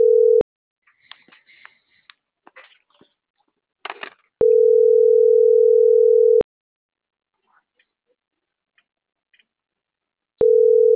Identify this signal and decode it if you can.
Ringback tone (North America), 440 + 480 Hz.